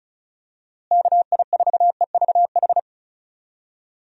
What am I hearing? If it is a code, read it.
Morse code, heard as a keyed tone, about 34 wpm, 700 Hz. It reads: KI4EVH
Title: USA Call Signs - 6 Characters - Repeated 3x - 35wpm